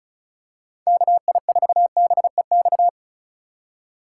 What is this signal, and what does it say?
Morse code, heard as a keyed tone, about 36 wpm, 700 Hz. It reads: KI4BEX
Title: USA Call Signs - 6 Characters - 4x Character Spacing - 35wpm